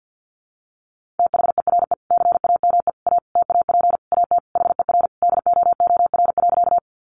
Morse code, heard as a keyed tone, about 50 wpm, 700 Hz. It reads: THERE CAME A TAP AT HER DOOR.
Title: Sentences from Top 1400 Words 50wpm